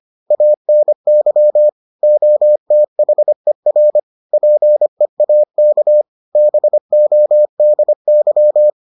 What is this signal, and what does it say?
Morse code, heard as a keyed tone, about 25 wpm, 600 Hz. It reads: ANY OTHER PEAK BODY